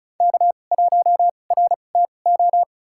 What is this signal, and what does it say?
Morse code, heard as a keyed tone, about 36 wpm, spaced as for 25 wpm, 700 Hz. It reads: K1RTO
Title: USA Call Signs - 5 Characters - 2x Character Spacing - 35wpm